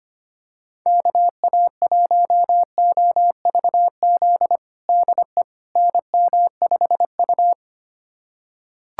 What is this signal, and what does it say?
Morse code, heard as a keyed tone, about 25 wpm, 700 Hz. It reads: KA1OVZ DE NM5U
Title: USA Callsign DE Callsign 25wpm